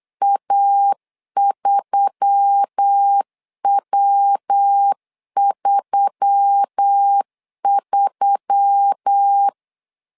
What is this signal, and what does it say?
Morse code, heard as a keyed tone, about 8 wpm, 800 Hz. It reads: A3W33
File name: LEZIONI / 4 Q1S3 8wpm 3037